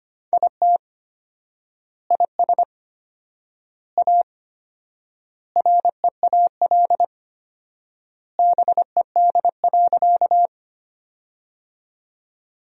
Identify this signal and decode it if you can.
Morse code, heard as a keyed tone, about 25 wpm, 700 Hz. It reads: IT IS A REAL BED.
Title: 4x Word-Spacing - Sentences from Top 500 Words - 25wpm